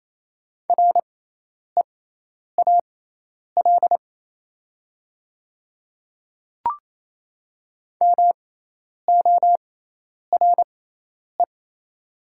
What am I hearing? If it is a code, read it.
Morse code, heard as a keyed tone, about 28 wpm, spaced as for 10 wpm, 700 Hz. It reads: REAL MORE
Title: 6x Character Spacing - Top 300 Words - 28wpm